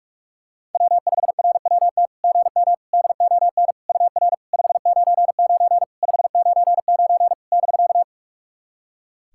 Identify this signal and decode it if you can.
Morse code, heard as a keyed tone, about 44 wpm, 690 Hz. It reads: W5RWT GM DON UR 599 599 <BK>
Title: SOTA QSO Practice 45wpm